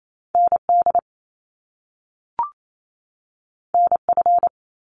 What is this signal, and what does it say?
Morse code, heard as a keyed tone, about 28 wpm, 700 Hz. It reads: ND NF
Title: Top Two Letter Combinations in Words 28wpm